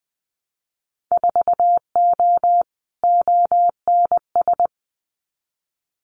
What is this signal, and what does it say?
Morse code, heard as a keyed tone, about 20 wpm, 700 Hz. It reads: 4O ONS
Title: Winter Field Day Exchanges 20wpm